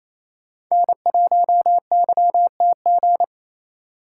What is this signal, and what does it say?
Morse code, heard as a keyed tone, about 28 wpm, 700 Hz. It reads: N1YTG
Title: USA Call Signs - 5 Characters - 2x Character Spacing - 28wpm